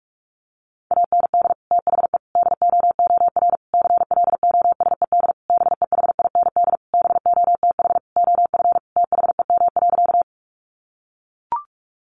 Morse code, heard as a keyed tone, about 44 wpm, 700 Hz. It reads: AND THE DOOR CLOSED BEHIND BOTH OF THEM.
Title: Sentences from Top 1400 Words 45wpm